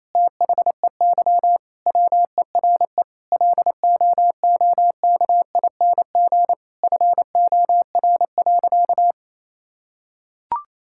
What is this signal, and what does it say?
Morse code, heard as a keyed tone, about 28 wpm, 700 Hz. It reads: THEY WERE LOOKING FOR.
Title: Sentences from Top 1000 Words - Review Part 6 - 28wpm